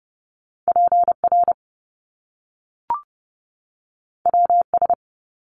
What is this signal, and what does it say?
Morse code, heard as a keyed tone, about 30 wpm, 700 Hz. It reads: PR WS
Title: Top Two Letter Combinations in Words 30wpm